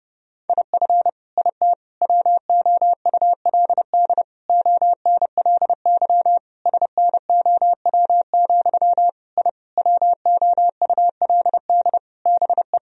Morse code, heard as a keyed tone, about 30 wpm, 700 Hz. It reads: IF IT WOULD ONLY SNOW, I WOULD BE
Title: Sentences from Top 1000 Words - Review Part 3 - 30wpm